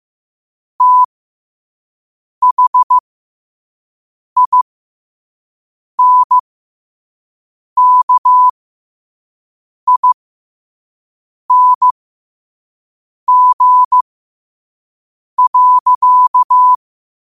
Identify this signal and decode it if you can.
Morse code, heard as a keyed tone, about 15 wpm, spaced as for 5 wpm, 1000 Hz. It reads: THINKING.
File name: qotdfiles / 5wpm